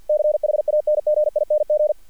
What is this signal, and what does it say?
Morse code, heard as a keyed tone, about 48 wpm, 600 Hz. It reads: CHANGING
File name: archived / CRSnet